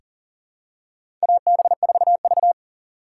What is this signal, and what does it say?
Morse code, heard as a keyed tone, about 37 wpm, 700 Hz. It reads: AB4V
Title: USA Call Signs - 4 Characters - 3x Character Spacing - 40wpm